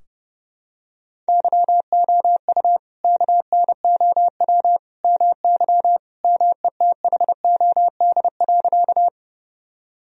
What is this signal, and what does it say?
Morse code, heard as a keyed tone, about 31 wpm, 700 Hz. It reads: YOU KNOW MY METHOD.
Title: Sentences from Top 1000 Words - Review Part 3 - 30wpm